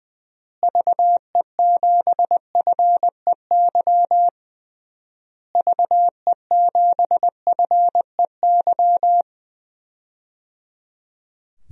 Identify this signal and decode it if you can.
Morse code, heard as a keyed tone, about 20 wpm, 700 Hz. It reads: VE7FEY VE7FEY
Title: DXCC-100 Call Signs - 6 Characters - Repeated 3x - 20wpm